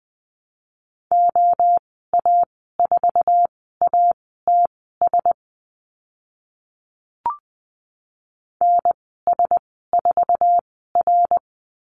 Morse code, heard as a keyed tone, about 20 wpm, spaced as for 15 wpm, 700 Hz. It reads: OA4ATS NS4R
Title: 076 Lesson 13g - Letter L - Call Signs 20wpm